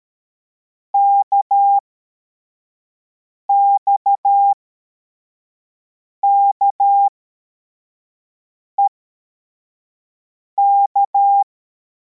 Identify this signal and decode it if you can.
Morse code, heard as a keyed tone, about 13 wpm, spaced as for 4 wpm, 800 Hz. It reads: KXKEK
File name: formation / cw / Text33